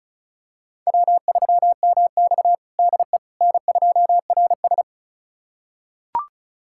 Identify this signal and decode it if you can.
Morse code, heard as a keyed tone, about 36 wpm, 700 Hz. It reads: W3MX DE N2RS
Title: USA Callsign DE Callsign - 4 Characters - 35wpm